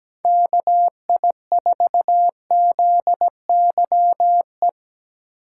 Morse code, heard as a keyed tone, about 17 wpm, 700 Hz. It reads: KI4ZYE
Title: USA Call Signs - 6 Characters - 3x Character Spacing - 17wpm